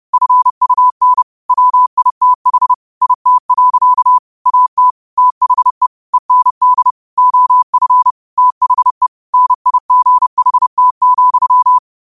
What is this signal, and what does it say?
Morse code, heard as a keyed tone, about 30 wpm, 1000 Hz. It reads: FUN WITH IT. AT THE END OF THE NIGHT,
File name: qotdfiles / 30wpm